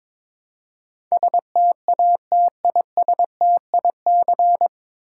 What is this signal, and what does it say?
Morse code, heard as a keyed tone, about 22 wpm, 700 Hz. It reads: STATISTIC
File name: 9 Letter Words - First 3 Letters Given - 22wpm